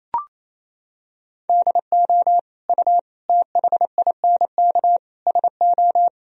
Morse code, heard as a keyed tone, about 28 wpm, 700 Hz. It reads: DO U THINK SO
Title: Sentences from Top 200 Words and Abbreviations 28wpm